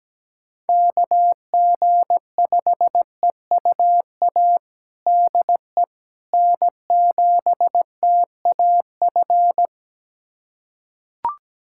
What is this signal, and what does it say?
Morse code, heard as a keyed tone, about 17 wpm, 700 Hz. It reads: KG5EUA DE N7TAF
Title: USA Callsign DE Callsign 17wpm